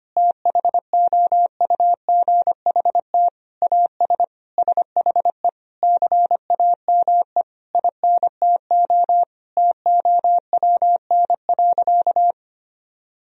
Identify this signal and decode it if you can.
Morse code, heard as a keyed tone, about 25 wpm, 700 Hz. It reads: THOUGHT AS SHE CAME INTO TOWN.
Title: Sentences from Top 1000 Words - Review Part 2 - 25wpm